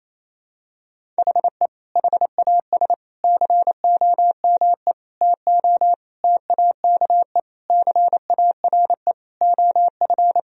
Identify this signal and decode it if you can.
Morse code, heard as a keyed tone, about 28 wpm, 700 Hz. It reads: HE HAS COME TO TAKE CARE OF
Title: Sentences from Top 1000 Words - Review Part 1 - 28wpm